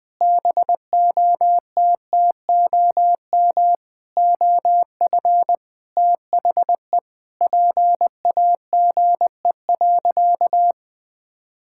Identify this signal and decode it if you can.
Morse code, heard as a keyed tone, about 21 wpm, 700 Hz. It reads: BOTTOM OF THE PAGE.
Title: Sentences with 4-Word Phrases - 20wpm